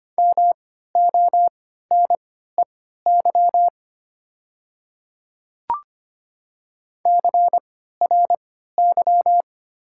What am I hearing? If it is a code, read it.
Morse code, heard as a keyed tone, about 25 wpm, spaced as for 14 wpm, 700 Hz. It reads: MONEY CRY
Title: 3x Character Spacing - Top 500 Words - 25wpm